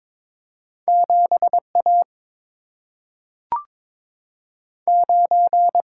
Morse code, heard as a keyed tone, about 22 wpm, 700 Hz. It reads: 7 A 9
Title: Single-Letter Single-Number 22wpm